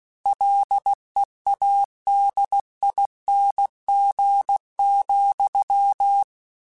Morse code, heard as a keyed tone, about 16 wpm, 800 Hz. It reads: LEADING,
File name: Combinations / Combinations-mp3s / Combinations@15